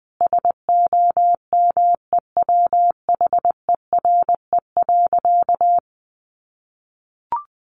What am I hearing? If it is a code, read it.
Morse code, heard as a keyed tone, about 20 wpm, 700 Hz. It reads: SOMEWHERE.
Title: Sentences from Top 1400 Words 20wpm